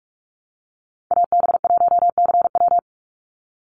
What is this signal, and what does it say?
Morse code, heard as a keyed tone, about 44 wpm, 700 Hz. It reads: AB1CW
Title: Call Signs Repeated 2x - 45wpm